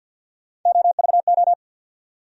Morse code, heard as a keyed tone, about 50 wpm, 700 Hz. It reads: OVO
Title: USA Call Sign Postfixes - 3 Letters 50wpm